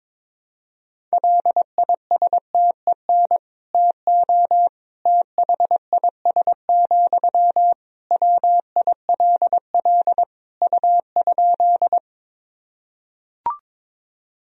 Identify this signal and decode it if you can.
Morse code, heard as a keyed tone, about 21 wpm, 700 Hz. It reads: LISTEN TO THIS, WILL U?
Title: Sentences from Top 400 Words and Abbreviations 22wpm